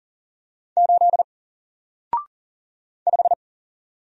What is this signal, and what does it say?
Morse code, heard as a keyed tone, about 40 wpm, 700 Hz. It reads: 8 5
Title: Single Letter-Number Alt 40wpm